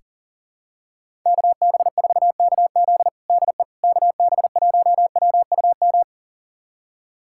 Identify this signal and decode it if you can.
Morse code, heard as a keyed tone, about 37 wpm, 700 Hz. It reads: KB4KZ DE KB1WUM
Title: USA Callsign DE Callsign 40wpm